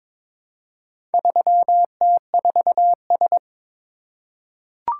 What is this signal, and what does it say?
Morse code, heard as a keyed tone, about 22 wpm, 700 Hz. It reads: 3T4S
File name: practice / 22wpm / Four Letter-Number Combinations - 22wpm